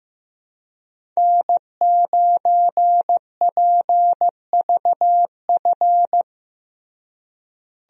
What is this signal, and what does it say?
Morse code, heard as a keyed tone, about 15 wpm, 700 Hz. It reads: N9PVF
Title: USA Call Signs - 5 Characters - Repeated 2x - 15wpm